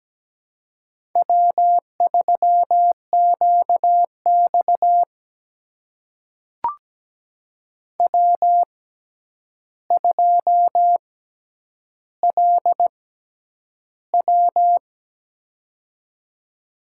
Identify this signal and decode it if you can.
Morse code, heard as a keyed tone, about 17 wpm, spaced as for 9 wpm, 700 Hz. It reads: W3QX W2LW
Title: USA Call Signs - 4 Characters - 6x Character Spacing - 17wpm